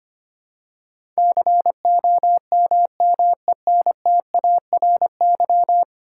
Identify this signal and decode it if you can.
Morse code, heard as a keyed tone, about 25 wpm, 700 Hz. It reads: COMMENTARY
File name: 10 Letter Words - First 3 Letters Given - 25wpm